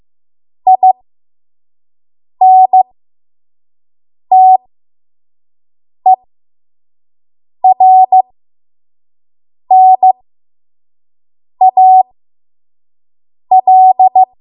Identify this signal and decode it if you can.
Morse code, heard as a keyed tone, about 15 wpm, spaced as for 5 wpm, 750 Hz. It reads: INTERNAL
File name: Morse / 05WPM